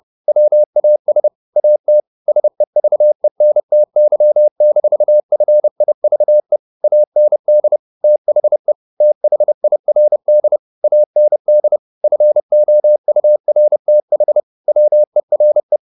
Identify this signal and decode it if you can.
Morse code, heard as a keyed tone, about 30 wpm, 600 Hz. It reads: WAS AT SEVENTY-FIVE AND THE THIRD AND FOURTH WERE
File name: the old man and the sea / the old man and the sea 1 30wpm